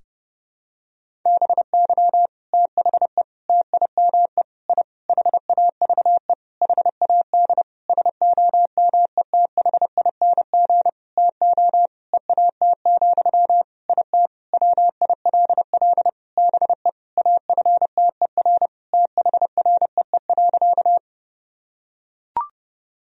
Morse code, heard as a keyed tone, about 30 wpm, 700 Hz. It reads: BY THE TIME I HAVE HAD SOMETHING TO EAT, IT WILL BE AFTER THREE.
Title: Sentences from Top 1000 Words - Review Part 1 - 30wpm